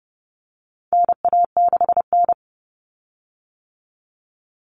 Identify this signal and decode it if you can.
Morse code, heard as a keyed tone, about 30 wpm, 700 Hz. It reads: NA6N
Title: Common Cut Numbers - 4-Digit 30wpm